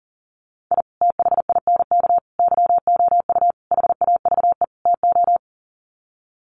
Morse code, heard as a keyed tone, about 40 wpm, 700 Hz. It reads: I THINK YOU HAVE TO
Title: Sets of 5 Words 40wpm